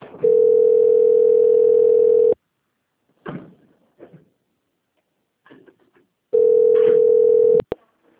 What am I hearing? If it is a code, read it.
Ringback tone (North America), 440 + 480 Hz.
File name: out- - - - - .6487